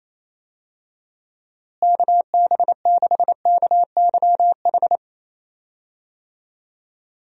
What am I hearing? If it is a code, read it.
Morse code, heard as a keyed tone, about 28 wpm, 700 Hz. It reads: KB6KYH
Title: USA Call Signs - 6 Characters - Repeated 2x - 28wpm